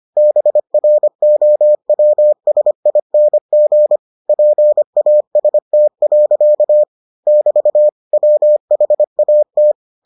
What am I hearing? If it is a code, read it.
Morse code, heard as a keyed tone, about 25 wpm, 600 Hz. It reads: BROWSING PAST. <BT> WHAT